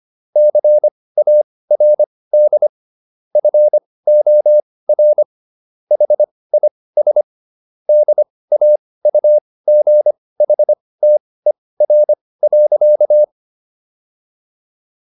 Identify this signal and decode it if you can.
Morse code, heard as a keyed tone, about 25 wpm, spaced as for 18 wpm, 600 Hz. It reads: CARD FOR HIS DAUGHTER.